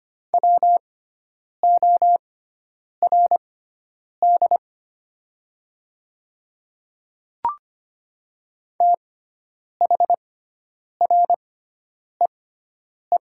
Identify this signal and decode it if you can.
Morse code, heard as a keyed tone, about 25 wpm, spaced as for 9 wpm, 700 Hz. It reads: WORD THREE
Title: 6x Character Spacing - Top 500 Words - 25wpm